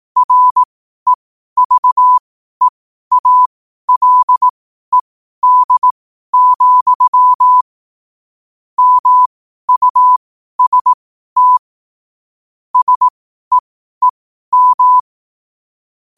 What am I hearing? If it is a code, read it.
Morse code, heard as a keyed tone, about 18 wpm, spaced as for 12 wpm, 1000 Hz. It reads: REVEALED, MUST SEEM
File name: qotdfiles / 12wpm